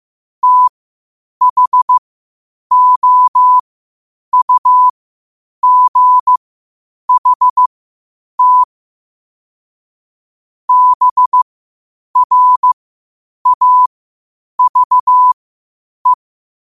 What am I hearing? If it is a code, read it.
Morse code, heard as a keyed tone, about 15 wpm, spaced as for 8 wpm, 1000 Hz. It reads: THOUGHT BRAVE
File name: qotdfiles / 8wpm